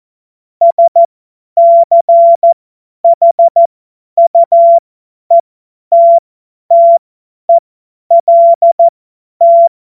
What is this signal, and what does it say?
Morse code, heard as a keyed tone, about 14 wpm, spaced as for 10 wpm, 680 Hz. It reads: SCHUETTELT
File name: Morsekurs / Witze / Witz 17.Oktober 2023 50 F70